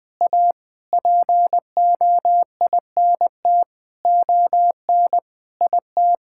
Morse code, heard as a keyed tone, about 21 wpm, 700 Hz. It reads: A POINT ON IT
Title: Sets of 9 Words - 20wpm